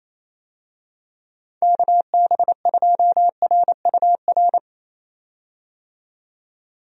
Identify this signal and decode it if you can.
Morse code, heard as a keyed tone, about 28 wpm, 700 Hz. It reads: KB2RUR
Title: USA Call Signs - 6 Characters - 28wpm